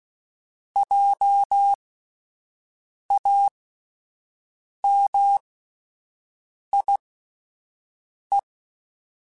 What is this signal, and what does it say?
Morse code, heard as a keyed tone, about 16 wpm, spaced as for 5 wpm, 800 Hz. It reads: JAMIE